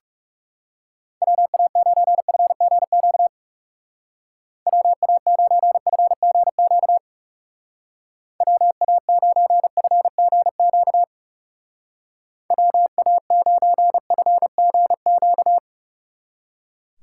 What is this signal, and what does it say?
Morse code, heard as a keyed tone, about 35 wpm, 700 Hz. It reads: WA9FGQ WA9FGQ WA9FGQ WA9FGQ
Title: Speed Racing - USA 2x3 Call Signs 30wpm